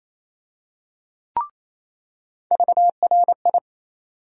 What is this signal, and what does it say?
Morse code, heard as a keyed tone, about 28 wpm, 700 Hz. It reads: VRI
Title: USA Call Sign Postfixes - 3 Letters 28wpm